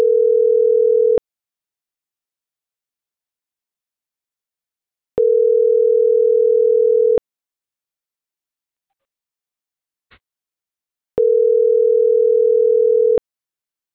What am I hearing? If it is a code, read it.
Ringback tone (North America), 440 + 480 Hz.
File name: CENTRAL SAGOT / out adriana.poncec